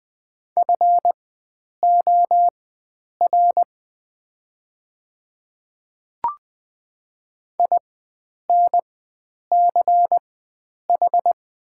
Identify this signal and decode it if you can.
Morse code, heard as a keyed tone, about 20 wpm, spaced as for 9 wpm, 690 Hz. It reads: FOR INCH
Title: 4x Character Spacing - Top 500 Words - 20wpm